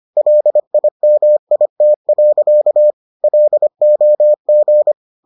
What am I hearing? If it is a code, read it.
Morse code, heard as a keyed tone, about 25 wpm, 600 Hz. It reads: LIMIT. LOG